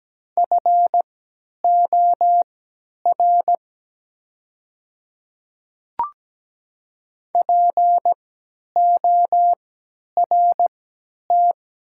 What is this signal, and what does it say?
Morse code, heard as a keyed tone, about 17 wpm, spaced as for 10 wpm, 700 Hz. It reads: FOR PORT
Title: 3x Character Spacing - Top 300 Words - 17wpm